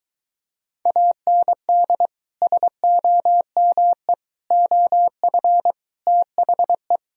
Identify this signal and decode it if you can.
Morse code, heard as a keyed tone, about 23 wpm, 700 Hz. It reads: AND SOME OF THE